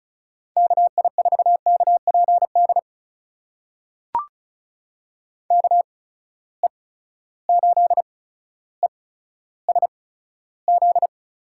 Morse code, heard as a keyed tone, about 36 wpm, spaced as for 15 wpm, 700 Hz. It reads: KI4KPD KE8ESZ
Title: USA Call Signs - 6 Characters - 8x Character Spacing - 35wpm